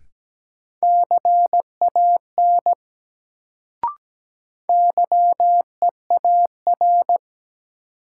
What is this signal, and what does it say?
Morse code, heard as a keyed tone, about 17 wpm, 700 Hz. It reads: CAN YEAR
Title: Sets of 2 Words - New Word Drill - 17wpm